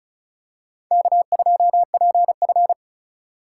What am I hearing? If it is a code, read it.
Morse code, heard as a keyed tone, about 34 wpm, 700 Hz. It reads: K2PF
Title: USA Call Signs - 4 Characters - 4x Character Spacing - 35wpm